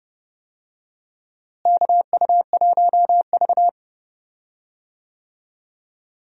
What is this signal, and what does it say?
Morse code, heard as a keyed tone, about 30 wpm, 700 Hz. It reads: KU1V
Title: USA Call Signs - 4 Characters - Repeated 3x - 30wpm